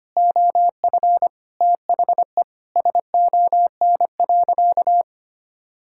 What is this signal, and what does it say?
Morse code, heard as a keyed tone, about 25 wpm, 700 Hz. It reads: OF THE SON.
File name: practice / 25wpm / Sentences with 9-Word Phrases - 25wpm